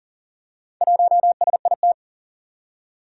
Morse code, heard as a keyed tone, about 38 wpm, 700 Hz. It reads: 1SIT
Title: Four Challenging Letter-Number Combinations 40wpm